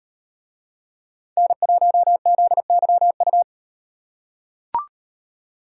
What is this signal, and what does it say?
Morse code, heard as a keyed tone, about 38 wpm, 700 Hz. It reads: N1ZYU